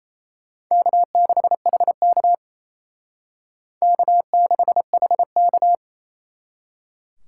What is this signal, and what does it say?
Morse code, heard as a keyed tone, about 32 wpm, 700 Hz. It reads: K6HK K6HK
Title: Speed Racing - USA Call Signs 28wpm